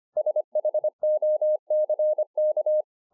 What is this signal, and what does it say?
Morse code, heard as a keyed tone, about 25 wpm, 610 Hz. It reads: SHOCK